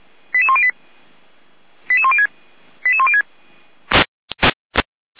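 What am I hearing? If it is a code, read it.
Five-tone selective call (ZVEI), heard as quick runs of five tones, tones of about 70 ms. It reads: 88108 88106 88106
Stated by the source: Five-tone ZVEI selective call received by SDR, from the audio archive of Feuerwehr Orlamünde (fire brigade).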